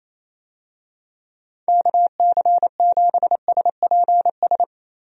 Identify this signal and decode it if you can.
Morse code, heard as a keyed tone, about 28 wpm, 700 Hz. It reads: KC7SPS
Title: USA Call Signs - 6 Characters - Repeated 3x - 28wpm